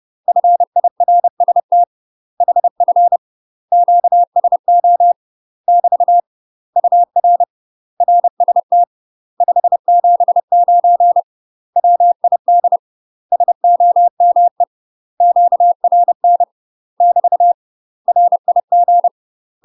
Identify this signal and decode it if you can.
Morse code, heard as a keyed tone, about 30 wpm, 700 Hz. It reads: FIRST HF QSO <BT> UR RST 579 WID SOME QRN <BT> RIG
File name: UK MASTER QSOs TRAINING / 30WPM / UK MASTER QSOs 0029